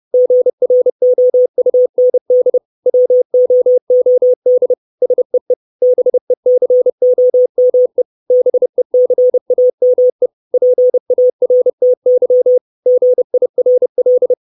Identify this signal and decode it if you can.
Morse code, heard as a keyed tone, about 30 wpm, 500 Hz. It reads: GROUND WOOD SEE BECOME BECAME PARTY GIRL
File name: 632 Words 30WPM